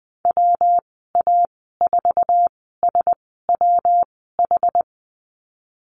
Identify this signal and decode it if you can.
Morse code, heard as a keyed tone, about 20 wpm, spaced as for 14 wpm, 700 Hz. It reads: WA4SWH
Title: 147 Lesson 24g - Letter y - Call Signs 20wpm